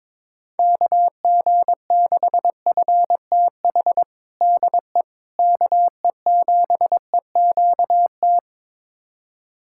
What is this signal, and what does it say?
Morse code, heard as a keyed tone, about 22 wpm, 700 Hz. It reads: KG6FTH DE KE7EQT
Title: USA Callsign DE Callsign - 6 Characters - 22wpm